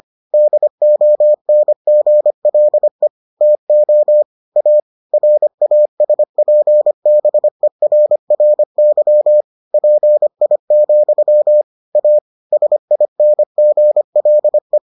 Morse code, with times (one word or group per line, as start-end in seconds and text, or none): 0.3-3.1 DONGLE
3.4-4.2 TO
4.6-4.8 A
5.1-9.4 RASPBERRY
9.7-11.6 PI,
11.9-12.2 A
12.5-14.8 SINGLE